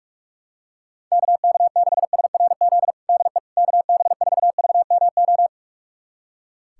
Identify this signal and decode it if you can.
Morse code, heard as a keyed tone, about 46 wpm, 700 Hz. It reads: KK6SRZ DE KB4VMO
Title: USA Callsign DE Callsign - 6 Characters - 45wpm